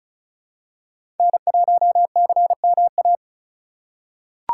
Morse code, heard as a keyed tone, about 34 wpm, 700 Hz. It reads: N1CMA
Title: USA Call Signs - 3x Character Spacing - 35wpm